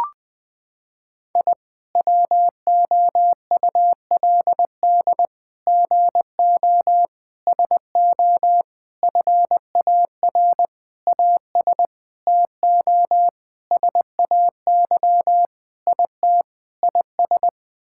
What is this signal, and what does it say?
Morse code, heard as a keyed tone, about 21 wpm, 700 Hz. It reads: I WOULD GO SO FAR AS TO SAY IT IS
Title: Sentences with 8-Word Phrases - 20wpm